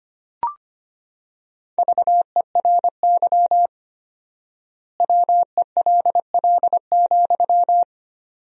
Morse code, heard as a keyed tone, about 25 wpm, 700 Hz. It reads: VERY WELL,
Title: 4x Word-Spacing - Sentences from Top 200 Words - 25wpm